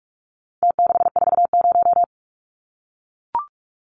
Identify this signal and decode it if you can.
Morse code, heard as a keyed tone, about 44 wpm, 700 Hz. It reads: T640
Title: Numbers - Contest Style - 4-Digit 45wpm